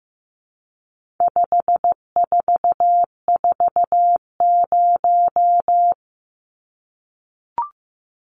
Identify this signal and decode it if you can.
Morse code, heard as a keyed tone, about 15 wpm, 700 Hz. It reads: 5440
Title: Numbers - 4-Digit 15wpm